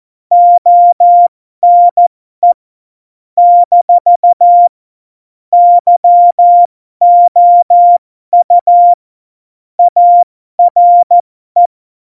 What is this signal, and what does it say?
Morse code, heard as a keyed tone, about 14 wpm, spaced as for 12 wpm, 700 Hz. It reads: ONE - YOU ARE
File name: Morsekurs / Witze